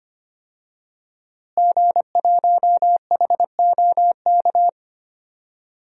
Morse code, heard as a keyed tone, about 25 wpm, 700 Hz. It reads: G1HOK
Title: DXCC-100 Call Signs - 5 Characters - Repeated 3x - 25wpm